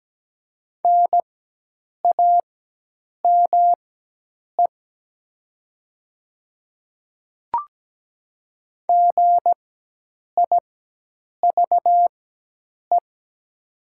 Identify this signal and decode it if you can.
Morse code, heard as a keyed tone, about 17 wpm, spaced as for 8 wpm, 700 Hz. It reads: NAME GIVE
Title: 4x Character Spacing - Top 200 Words - 17wpm